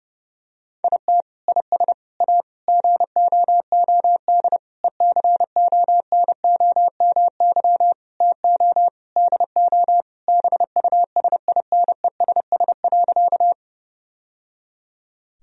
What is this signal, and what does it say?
Morse code, heard as a keyed tone, about 30 wpm, 700 Hz. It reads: IT IS A GOOD ECONOMY TO DO BUSINESS.
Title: Sentences from Top 1000 Words - Review Part 3 - 30wpm